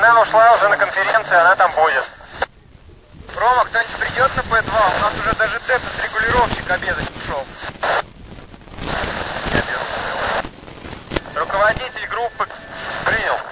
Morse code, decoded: none

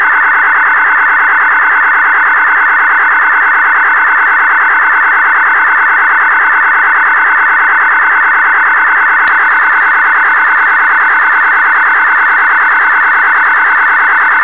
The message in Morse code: none

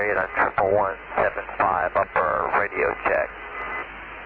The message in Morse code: none

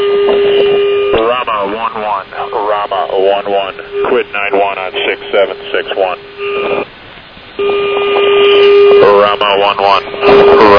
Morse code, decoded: none

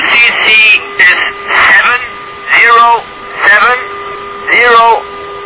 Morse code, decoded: none

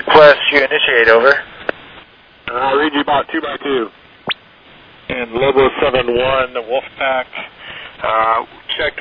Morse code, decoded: none

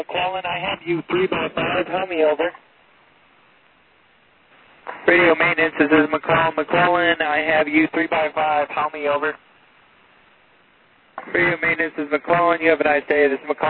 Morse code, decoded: none

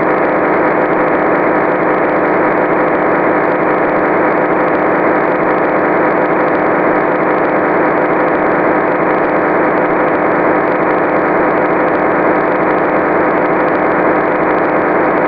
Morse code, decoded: none

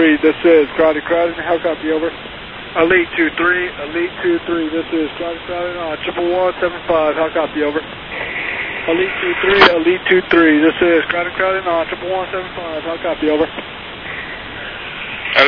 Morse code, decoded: none

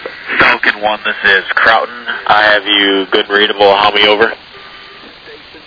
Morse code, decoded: none

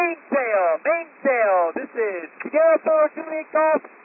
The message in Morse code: none

